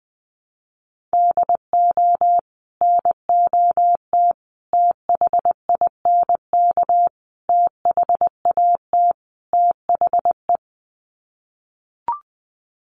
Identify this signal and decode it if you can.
Morse code, heard as a keyed tone, about 20 wpm, 700 Hz. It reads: DO NOT THINK THAT THE
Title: Sets of 5 Words Encore2 20wpm